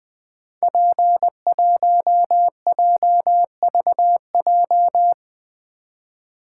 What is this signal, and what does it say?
Morse code, heard as a keyed tone, about 20 wpm, 700 Hz. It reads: P1JVJ
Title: Code Groups - 5 Letter:Numbers - 20wpm